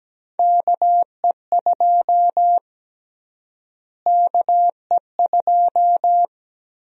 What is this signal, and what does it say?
Morse code, heard as a keyed tone, about 17 wpm, 700 Hz. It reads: KE2 KE2
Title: USA Call Sign Prefixes - Repeated 2x - 17wpm